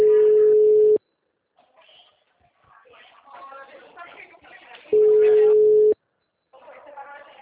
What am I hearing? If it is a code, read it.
Ringback tone (Europe, 425 Hz), with speech.